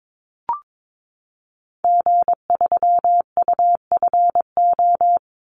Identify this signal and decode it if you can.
Morse code, heard as a keyed tone, about 22 wpm, 700 Hz. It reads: G3UFO
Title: Call Signs Repeated 3x - 22wpm